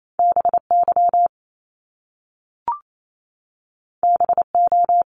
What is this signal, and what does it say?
Morse code, heard as a keyed tone, about 28 wpm, 700 Hz. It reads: BY BO
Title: Top Two Letter Combinations in Words 28wpm